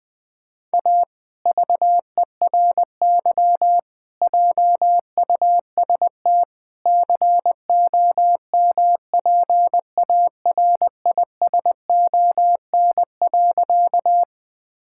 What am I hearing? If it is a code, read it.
Morse code, heard as a keyed tone, about 20 wpm, 700 Hz. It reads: A VERY JUST COMPARISON.
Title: Sentences from Top 1600 Words 20wpm